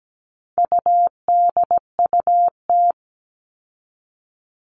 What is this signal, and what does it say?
Morse code, heard as a keyed tone, about 17 wpm, 700 Hz. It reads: UDUT
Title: All Cut Numbers - 4-Digit 17wpm